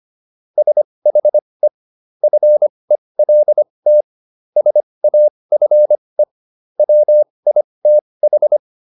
Morse code, heard as a keyed tone, about 25 wpm, spaced as for 20 wpm, 600 Hz. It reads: SHE FELT SAFE WITH